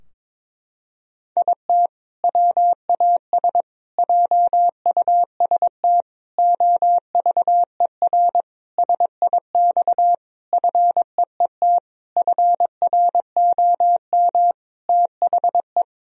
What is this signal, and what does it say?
Morse code, heard as a keyed tone, about 22 wpm, 700 Hz. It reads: IT WAS JUST OVER SIX FEET FROM THE
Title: Sentences from Top 1000 Words - Review Part 2 - 22wpm